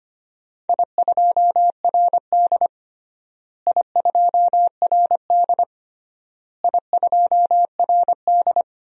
Morse code, heard as a keyed tone, about 25 wpm, 700 Hz. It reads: I2RD I2RD I2RD
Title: DXCC-100 Call Signs - 4 Characters - Repeated 3x - 25wpm